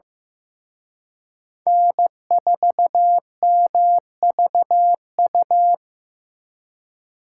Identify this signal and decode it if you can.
Morse code, heard as a keyed tone, about 18 wpm, 700 Hz. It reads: N4MVU
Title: Speed Racing - USA 1x3 Call Signs 15wpm